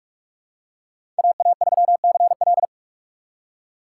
Morse code, heard as a keyed tone, about 44 wpm, 690 Hz. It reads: AA3CL
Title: USA Call Signs - 5 Characters - 6x Character Spacing - 45wpm